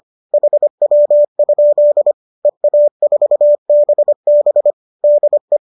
Morse code, heard as a keyed tone, about 25 wpm, 570 Hz. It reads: HW? EA4BB DE